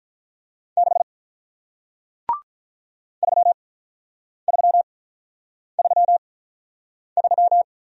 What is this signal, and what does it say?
Morse code, heard as a keyed tone, about 46 wpm, spaced as for 9 wpm, 700 Hz. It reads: 6 3333